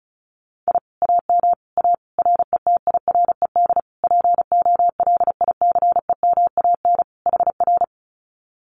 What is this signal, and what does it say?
Morse code, heard as a keyed tone, about 34 wpm, 700 Hz. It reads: I AM A RETIRED POLICEMAN HR
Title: Common QSO Phrases Encore 35wpm